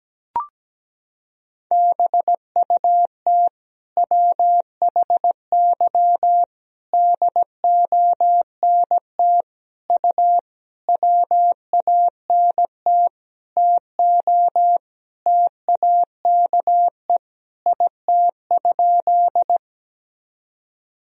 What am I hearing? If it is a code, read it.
Morse code, heard as a keyed tone, about 17 wpm, 700 Hz. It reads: BUT WHY DONT U WANT TO TAKE IT?
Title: Sentences from Top 300 Words and Abbreviations 17wpm